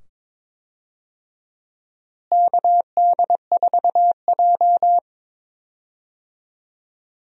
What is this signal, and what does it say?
Morse code, heard as a keyed tone, about 22 wpm, 700 Hz. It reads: KD4J
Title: USA Call Signs - 4 Characters - Repeated 2x - 22wpm